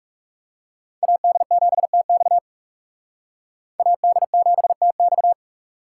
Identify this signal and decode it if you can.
Morse code, heard as a keyed tone, about 39 wpm, 700 Hz. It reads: AD7TX AD7TX